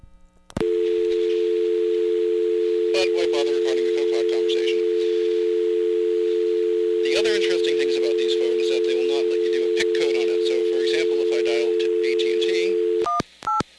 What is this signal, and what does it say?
Dial tone (North America), 350 + 440 Hz, with speech.
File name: blackratchet / recordings / charles-st-payphone